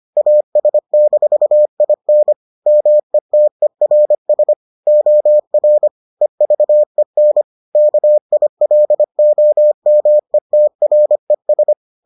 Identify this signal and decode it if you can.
Morse code, heard as a keyed tone, about 25 wpm, 600 Hz. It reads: AS-IN METERS OR EVEN KILOMETRES